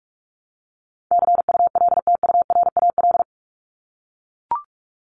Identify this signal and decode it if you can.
Morse code, heard as a keyed tone, about 44 wpm, 700 Hz. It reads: CULTURAL
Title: Top 1251-1500 Words 45wpm